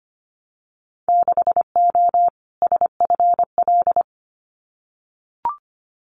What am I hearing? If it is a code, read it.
Morse code, heard as a keyed tone, about 25 wpm, 690 Hz. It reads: 6O SFL
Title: Winter Field Day Exchanges Encore 25wpm